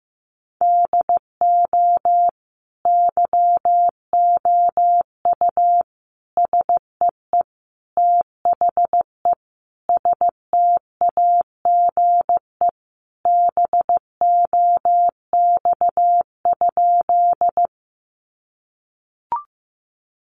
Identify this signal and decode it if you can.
Morse code, heard as a keyed tone, about 15 wpm, 700 Hz. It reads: DO YOU SEE THE STAGE BOX?
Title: Sentences from Top 1200 Words 15wpm